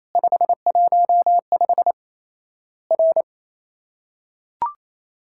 Morse code, heard as a keyed tone, about 28 wpm, 700 Hz. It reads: 515 R
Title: Worked All Europe DX Contest - QTC Exchange - 28wpm